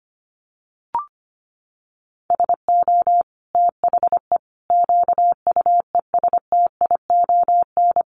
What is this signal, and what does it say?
Morse code, heard as a keyed tone, about 25 wpm, 700 Hz. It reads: SO THE QUESTION
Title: Sentences from Top 400 Words 25wpm